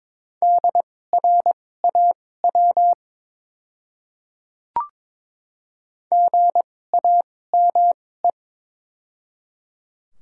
Morse code, heard as a keyed tone, about 22 wpm, spaced as for 16 wpm, 700 Hz. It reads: DRAW GAME
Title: 2x Character Spacing - Top 500 Words - 22wpm